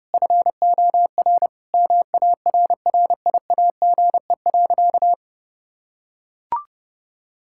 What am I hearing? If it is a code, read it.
Morse code, heard as a keyed tone, about 30 wpm, 700 Hz. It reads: FOR MARRIAGE.
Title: Sentences from Top 1000 Words - Review Part 6 - 30wpm